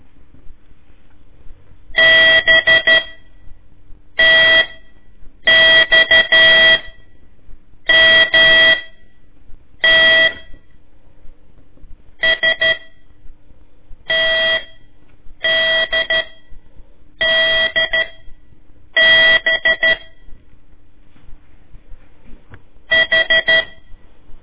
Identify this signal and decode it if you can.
Morse code, heard as a keyed tone, about 13 wpm, spaced as for 6 wpm, 1950 Hz. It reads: BTXMT STDDB H
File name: ik1weg2 / CORSI CW / lez4-1